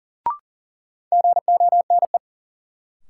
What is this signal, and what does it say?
Morse code, heard as a keyed tone, about 40 wpm, 700 Hz. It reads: GONE
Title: Sentences from Top 1200 Words - New Word Drill Alt - 40wpm